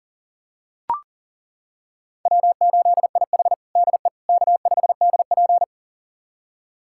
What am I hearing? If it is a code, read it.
Morse code, heard as a keyed tone, about 40 wpm, 700 Hz. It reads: W8IH DE K5DP
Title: USA Callsign DE Callsign - 4 Characters - 40wpm